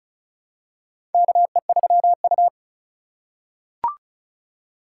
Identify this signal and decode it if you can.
Morse code, heard as a keyed tone, about 36 wpm, 700 Hz. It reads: KE3U
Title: USA Call Signs - 3x Character Spacing - 35wpm